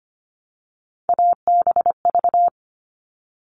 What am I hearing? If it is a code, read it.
Morse code, heard as a keyed tone, about 25 wpm, 700 Hz. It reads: ABV
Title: All Cut Numbers - 3-Digit 25wpm